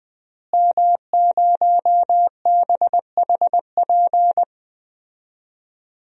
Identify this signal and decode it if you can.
Morse code, heard as a keyed tone, about 20 wpm, 700 Hz. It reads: M0BHP
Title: DXCC-100 Call Signs - 5 Characters - Repeated 3x - 20wpm